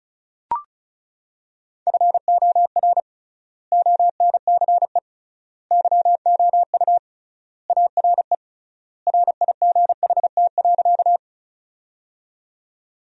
Morse code, heard as a keyed tone, about 36 wpm, 700 Hz. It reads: FOR ONCE YOU ARE RIGHT.
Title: 3x Word-Spacing - Sentences from Top 300 Words - 35wpm